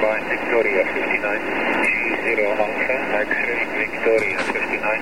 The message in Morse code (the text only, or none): none